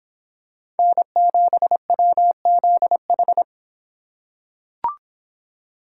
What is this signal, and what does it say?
Morse code, heard as a keyed tone, about 25 wpm, 700 Hz. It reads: N7WZH